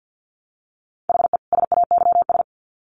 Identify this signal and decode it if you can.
Morse code, heard as a keyed tone, about 50 wpm, 700 Hz. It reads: HE SAYS
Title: Sets of 2 Words 50wpm